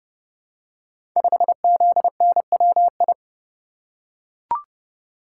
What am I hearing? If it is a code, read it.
Morse code, heard as a keyed tone, about 30 wpm, 700 Hz. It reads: 5ZNWI
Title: Code Groups - 5 Letter:Numbers - 30wpm